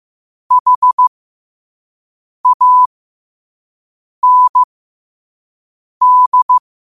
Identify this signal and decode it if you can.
Morse code, heard as a keyed tone, about 15 wpm, spaced as for 5 wpm, 1000 Hz. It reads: HAND
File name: qotdfiles / 5wpm